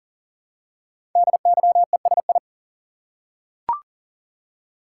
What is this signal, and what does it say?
Morse code, heard as a keyed tone, about 39 wpm, 700 Hz. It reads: DYESI
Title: Code Groups - 5 Letters - 40wpm